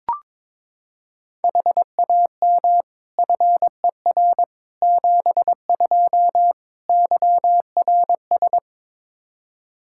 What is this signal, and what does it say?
Morse code, heard as a keyed tone, about 22 wpm, 700 Hz. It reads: HAM FER 72 YRS
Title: K1USN SST QSO Practice 22wpm